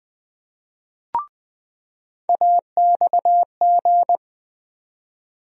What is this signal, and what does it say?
Morse code, heard as a keyed tone, about 20 wpm, 700 Hz. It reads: AXG
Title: USA Call Sign Postfixes - 3 Letters 20wpm